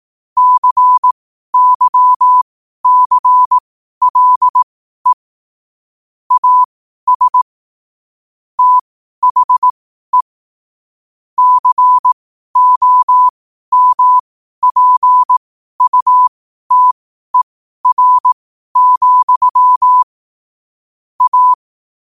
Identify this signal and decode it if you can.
Morse code, heard as a keyed tone, about 18 wpm, spaced as for 12 wpm, 1000 Hz. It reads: CYCLE AS THE COMPUTER, A
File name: qotdfiles / 12wpm